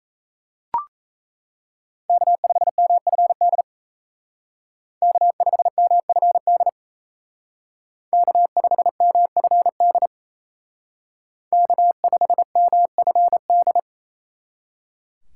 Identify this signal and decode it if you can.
Morse code, heard as a keyed tone, about 36 wpm, 700 Hz. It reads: K5MFD K5MFD K5MFD K5MFD